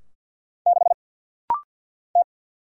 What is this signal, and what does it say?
Morse code, heard as a keyed tone, about 48 wpm, 700 Hz. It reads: 6 T
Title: Single Letter-Number - Rapid-Fire 50wpm